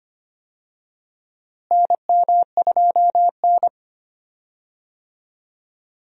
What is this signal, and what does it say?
Morse code, heard as a keyed tone, about 25 wpm, 700 Hz. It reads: NM2N
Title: USA Call Signs - 4 Characters - Repeated 3x - 25wpm